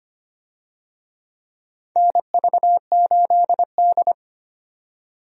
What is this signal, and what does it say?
Morse code, heard as a keyed tone, about 25 wpm, 700 Hz. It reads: NV8D
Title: USA Call Signs - 4 Characters - 25wpm